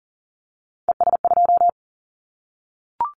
Morse code, heard as a keyed tone, about 39 wpm, 700 Hz. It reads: ES2
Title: Callsign Prefixes for Top 50 DXCC - 40wpm